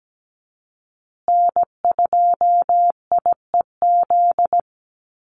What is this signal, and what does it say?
Morse code, heard as a keyed tone, about 17 wpm, 700 Hz. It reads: N2IEZ
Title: USA Call Signs Repeated 3x 17wpm